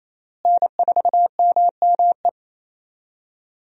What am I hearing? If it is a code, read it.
Morse code, heard as a keyed tone, about 28 wpm, 700 Hz. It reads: N4MME
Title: USA Call Signs - 5 Characters - Repeated 3x - 28wpm